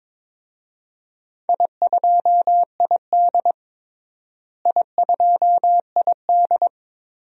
Morse code, heard as a keyed tone, about 22 wpm, 700 Hz. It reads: I2ID I2ID
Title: DXCC-100 Call Signs - 4 Characters - Repeated 3x - 22wpm